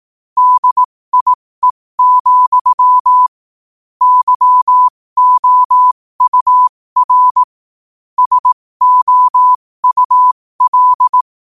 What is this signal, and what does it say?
Morse code, heard as a keyed tone, about 18 wpm, spaced as for 15 wpm, 1000 Hz. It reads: DIE, YOUR SOUL
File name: qotdfiles / 15wpm